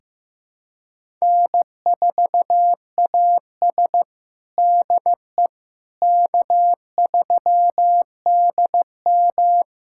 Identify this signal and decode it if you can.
Morse code, heard as a keyed tone, about 16 wpm, 700 Hz. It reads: N4AS DE K3DM
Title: USA Callsign DE Callsign - 4 Characters - 15wpm